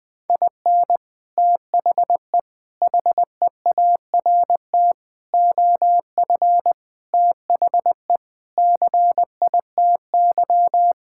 Morse code, heard as a keyed tone, about 20 wpm, 700 Hz. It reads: IN THE HEART OF THE CITY
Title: Sentences with 5-Word Phrases - 20wpm